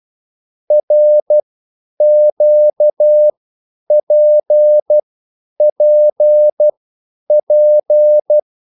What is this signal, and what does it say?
Morse code, heard as a keyed tone, about 12 wpm, spaced as for 9 wpm, 600 Hz. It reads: RQPPP
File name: QSL / CW lesson